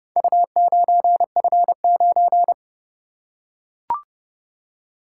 Morse code, heard as a keyed tone, about 29 wpm, 700 Hz. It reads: U9F9